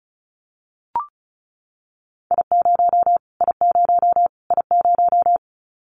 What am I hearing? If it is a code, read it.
Morse code, heard as a keyed tone, about 34 wpm, 700 Hz. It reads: I0 I0 I0
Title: Callsign Prefixes for Top 25 DXCC - Repeated 3x - 35wpm